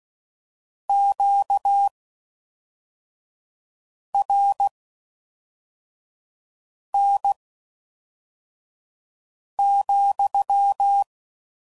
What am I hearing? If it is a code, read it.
Morse code, heard as a keyed tone, about 16 wpm, spaced as for 4 wpm, 800 Hz. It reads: QRN,